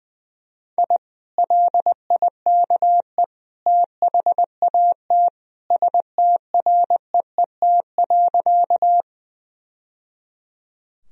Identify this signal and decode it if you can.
Morse code, heard as a keyed tone, about 21 wpm, 700 Hz. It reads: I LIKE THAT STREET.
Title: Sentences from Top 1000 Words - Review Part 3 - 20wpm